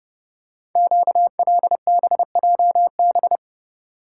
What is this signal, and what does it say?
Morse code, heard as a keyed tone, about 28 wpm, 700 Hz. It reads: QLBJB
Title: Code Groups - 5 Letters - 30wpm